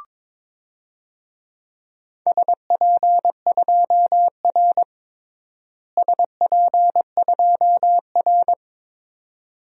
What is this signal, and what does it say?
Morse code, heard as a keyed tone, about 22 wpm, 700 Hz. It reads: SP2R SP2R
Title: DXCC-100 Call Signs - 4 Characters - Repeated 2x - 22wpm